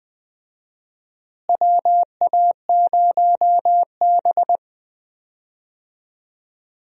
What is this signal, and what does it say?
Morse code, heard as a keyed tone, about 20 wpm, 700 Hz. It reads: WA0B
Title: USA Call Signs - 4 Characters - Repeated 3x - 20wpm